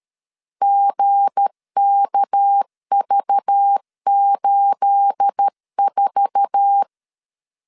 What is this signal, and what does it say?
Morse code, heard as a keyed tone, about 13 wpm, 800 Hz. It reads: GKV84
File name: CW / DET / 1 9 1500C S A 12wpm 3800